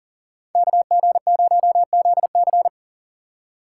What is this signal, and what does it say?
Morse code, heard as a keyed tone, about 40 wpm, 700 Hz. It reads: KG0ZC
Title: USA Call Signs - 5 Characters - 3x Character Spacing - 40wpm